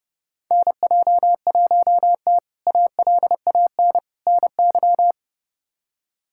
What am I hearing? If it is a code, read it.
Morse code, heard as a keyed tone, about 30 wpm, 700 Hz. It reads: NJ1T ALAN NY
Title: K1USN SST QSO Practice - 30wpm